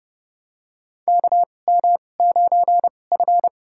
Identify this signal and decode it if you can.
Morse code, heard as a keyed tone, about 30 wpm, spaced as for 22 wpm, 700 Hz. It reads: KM9F
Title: USA Call Signs - 4 Characters - 2x Character Spacing - 30wpm